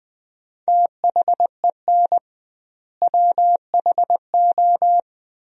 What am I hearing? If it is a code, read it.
Morse code, heard as a keyed tone, about 21 wpm, 700 Hz. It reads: THEN WHO
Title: 2x Word-Spacing - Sentences from Top 100 Words - 20wpm